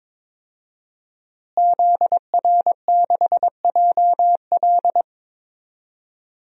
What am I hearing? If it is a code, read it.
Morse code, heard as a keyed tone, about 21 wpm, 700 Hz. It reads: ZR6JL
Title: DXCC-100 Call Signs - 5 Characters - Repeated 2x - 22wpm